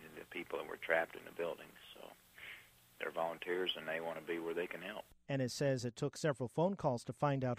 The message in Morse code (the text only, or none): none